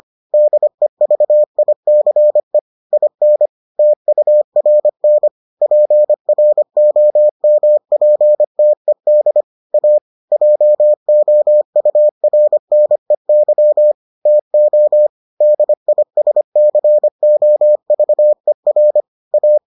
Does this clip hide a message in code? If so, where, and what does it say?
Yes: Morse code from 0.3 to 19.6 s, heard as a keyed tone, about 25 wpm, 600 Hz. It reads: DEVICE IN TURN PROMPTED A JOURNEY TO DISCOVER A